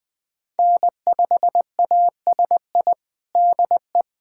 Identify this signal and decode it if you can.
Morse code, heard as a keyed tone, about 20 wpm, 700 Hz. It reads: N5ASI DE
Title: USA Callsign DE Callsign 20wpm